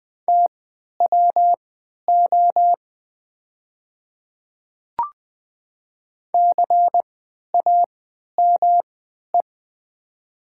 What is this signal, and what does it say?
Morse code, heard as a keyed tone, about 20 wpm, spaced as for 11 wpm, 700 Hz. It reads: TWO CAME
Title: 3x Character Spacing - Top 200 Words - 20wpm